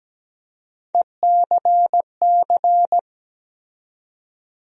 Morse code, heard as a keyed tone, about 17 wpm, 700 Hz. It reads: ECC
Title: USA Call Sign Postfixes - 3 Letters 17wpm